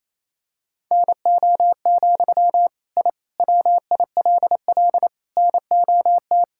Morse code, heard as a keyed tone, about 28 wpm, 700 Hz. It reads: NO, I WILL NOT
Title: Sentences from Top 1000 Words - Review Part 6 - 28wpm